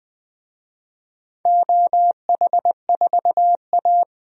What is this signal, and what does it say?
Morse code, heard as a keyed tone, about 20 wpm, 700 Hz. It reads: OH4A
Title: DXCC-100 Call Signs - 4 Characters - 20wpm